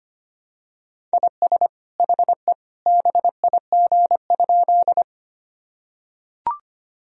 Morse code, heard as a keyed tone, about 25 wpm, 700 Hz. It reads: IS HE BIG?
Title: Sentences from Top 1000 Words - Review Part 1 - 25wpm